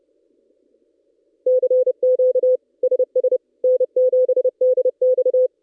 Morse code, heard as a keyed tone, about 30 wpm, 510 Hz. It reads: CQ SS N7DX
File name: contests / ss-cw-k5tr-op-n5rz -1700